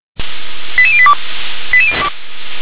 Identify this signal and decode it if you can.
Five-tone selective call (ZVEI), heard as quick runs of five tones, tones of about 70 ms. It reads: 88082
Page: /357/SDR/ZVEI/